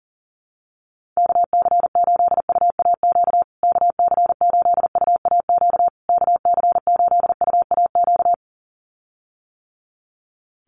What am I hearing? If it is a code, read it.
Morse code, heard as a keyed tone, about 40 wpm, 700 Hz. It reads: KC8UAQ KC8UAQ KC8UAQ
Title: USA Call Signs Repeated 3x 40wpm